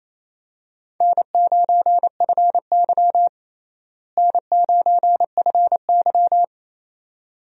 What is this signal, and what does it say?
Morse code, heard as a keyed tone, about 28 wpm, 700 Hz. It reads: N9FY N9FY
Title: USA Call Signs - 4 Characters - Repeated 2x - 28wpm